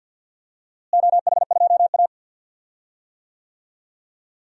Morse code, heard as a keyed tone, about 46 wpm, 700 Hz. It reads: OH2A
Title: DXCC-100 Call Signs - 4 Characters - Repeated 3x - 50wpm